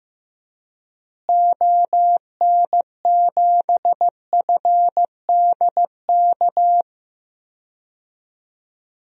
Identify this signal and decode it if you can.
Morse code, heard as a keyed tone, about 15 wpm, 700 Hz. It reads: ON7FDK
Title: DXCC-100 Call Signs - 6 Characters - Repeated 3x - 15wpm